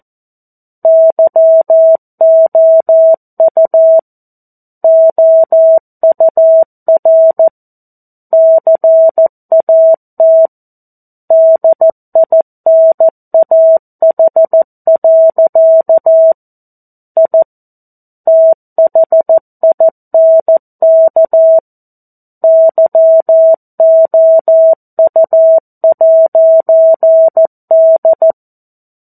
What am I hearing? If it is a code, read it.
Morse code, heard as a keyed tone, about 14 wpm, 650 Hz. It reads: YOU OUR CAT DINAH. I THINK YOU'D
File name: 02-Chapter-02 14 WPM